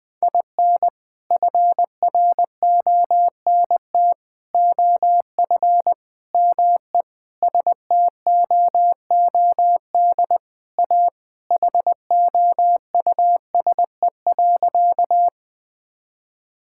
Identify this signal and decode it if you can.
Morse code, heard as a keyed tone, about 20 wpm, 700 Hz. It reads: IN FRONT OF ME STOOD A HOUSE.
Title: Sentences with 4-Word Phrases - Spoken Beginning - 20wpm